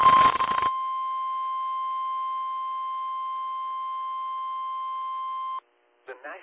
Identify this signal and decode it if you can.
Emergency Alert System (SAME) tones (the 1050 Hz attention tone), with speech. It is heard from 0.4 to 5.6 s.
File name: eas / KIMB / KIMB